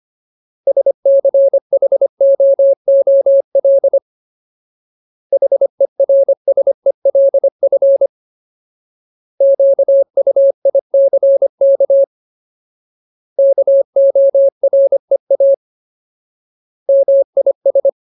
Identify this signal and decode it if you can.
Morse code, heard as a keyed tone, about 25 wpm, 550 Hz. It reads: SCHOOL HERSELF QUICK KOREA MIS
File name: Ultimate Word List / 25WPM Ultimate